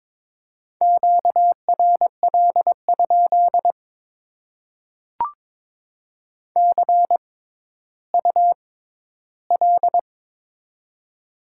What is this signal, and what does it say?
Morse code, heard as a keyed tone, about 22 wpm, spaced as for 12 wpm, 700 Hz. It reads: QRL? CUL
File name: Top 90 QSO Elements - 6x Character Spacing - 22wpm